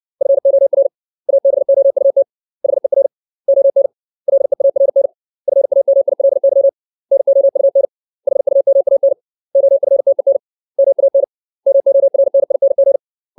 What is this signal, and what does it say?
Morse code, heard as a keyed tone, about 60 wpm, 550 Hz. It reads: FOR ABOUT HER ON LEARN FAMILY NOUN HUMAN OFTEN MAN MORNING
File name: 632 words / 632 Words 60WPM